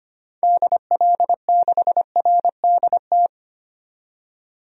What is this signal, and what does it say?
Morse code, heard as a keyed tone, about 25 wpm, 700 Hz. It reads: DL6RDT
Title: DXCC-100 Call Signs - 6 Characters - 25wpm